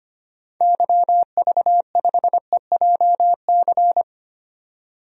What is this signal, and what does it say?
Morse code, heard as a keyed tone, about 25 wpm, 700 Hz. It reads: YV5EJC
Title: DXCC-100 Call Signs - 6 Characters - 25wpm